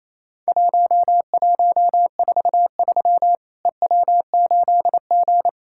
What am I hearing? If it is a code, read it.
Morse code, heard as a keyed tone, about 28 wpm, 700 Hz. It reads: 1143 EW8G
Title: Worked All Europe DX Contest - QTC Exchange - 28wpm